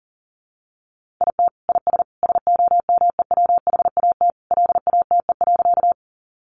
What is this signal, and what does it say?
Morse code, heard as a keyed tone, about 37 wpm, 700 Hz. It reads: IT IS SOMEWHAT LATE.
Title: Sentences from Top 1300 Words 40wpm